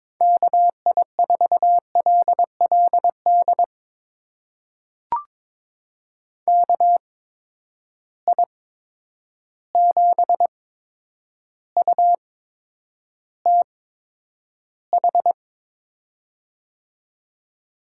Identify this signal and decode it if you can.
Morse code, heard as a keyed tone, about 22 wpm, spaced as for 9 wpm, 700 Hz. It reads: KI4LLD KI7UTH
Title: USA Call Signs - 6 Characters - 8x Character Spacing - 22wpm